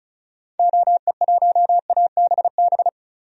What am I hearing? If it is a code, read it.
Morse code, heard as a keyed tone, about 36 wpm, 700 Hz. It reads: OE1ABB
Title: DXCC-100 Call Signs - 6 Characters - 35wpm